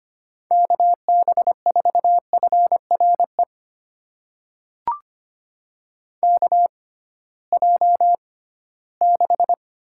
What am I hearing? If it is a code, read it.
Morse code, heard as a keyed tone, about 25 wpm, spaced as for 16 wpm, 700 Hz. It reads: KB4FRE KJ6
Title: USA Call Signs - 6x Character Spacing - 25wpm